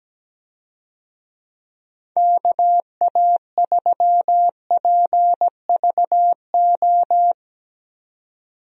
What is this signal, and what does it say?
Morse code, heard as a keyed tone, about 17 wpm, 700 Hz. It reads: KA3PVO
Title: USA Call Signs - 6 Characters - 17wpm